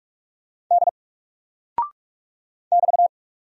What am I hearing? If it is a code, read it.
Morse code, heard as a keyed tone, about 44 wpm, 700 Hz. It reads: D <BT>
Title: Single Character (Letter:Number:Punctuation:Prosign) Alt 45wpm